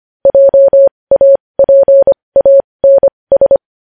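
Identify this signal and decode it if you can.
Morse code, heard as a keyed tone, about 25 wpm, spaced as for 20 wpm, 550 Hz. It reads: JAPANS
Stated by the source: Morse code news headlines sent at 20 wpm.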